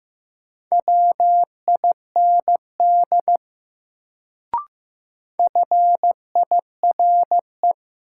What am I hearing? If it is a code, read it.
Morse code, heard as a keyed tone, about 15 wpm, 700 Hz. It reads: WIND FIRE
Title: Sentences from Top 400 Words - New Word Drill - 15wpm